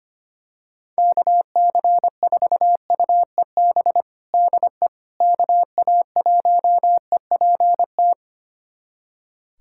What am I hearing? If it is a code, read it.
Morse code, heard as a keyed tone, about 25 wpm, 700 Hz. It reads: KC4UEB DE KA1EPT
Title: USA Callsign DE Callsign - 6 Characters - 25wpm